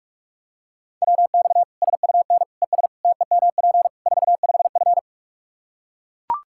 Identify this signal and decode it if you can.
Morse code, heard as a keyed tone, about 42 wpm, 700 Hz. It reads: WX SUN ES TEMP 45F
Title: Practice QSO - Exchange 2 - Info - 45wpm